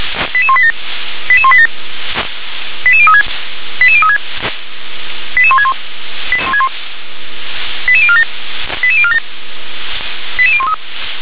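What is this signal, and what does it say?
Five-tone selective call (ZVEI), heard as quick runs of five tones, tones of about 70 ms. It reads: 88187 88187 88036 88036 88161 88047 88047 88013
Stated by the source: ZVEI five-tone call received by SDR.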